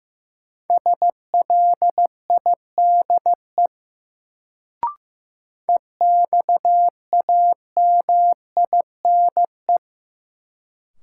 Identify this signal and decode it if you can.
Morse code, heard as a keyed tone, about 15 wpm, 700 Hz. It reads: SLIDE EXAMINE
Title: Sentences from Top 1600 Words - New Word Drill - 15wpm